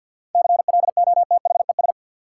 Morse code, heard as a keyed tone, about 50 wpm, 700 Hz. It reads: CLOTHES
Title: Seven Letter Words - 50wpm